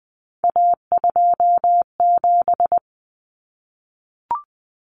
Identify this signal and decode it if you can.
Morse code, heard as a keyed tone, about 20 wpm, 700 Hz. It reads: A27
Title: Common Cut Numbers - 3-Digit 20wpm